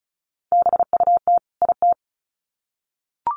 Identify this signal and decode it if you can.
Morse code, heard as a keyed tone, about 34 wpm, 700 Hz. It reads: BUT IT
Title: Sets of 2 Words 35wpm